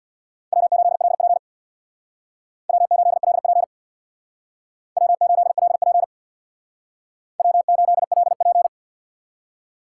Morse code, heard as a keyed tone, about 62 wpm, 700 Hz. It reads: W8LP W8LP W8LP W8LP
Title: Speed Racing - USA 1x2 Call Signs 50wpm